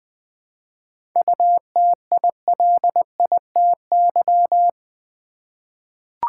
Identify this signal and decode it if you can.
Morse code, heard as a keyed tone, about 20 wpm, 690 Hz. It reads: UTILITY
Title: 7 Letter Words - First 3 Letters Given - 20wpm